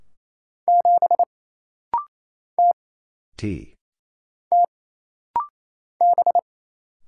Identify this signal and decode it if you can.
Morse code, heard as a keyed tone, about 28 wpm, 700 Hz. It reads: 7 T T B
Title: Single Letter-Number - Rapid-Fire 28wpm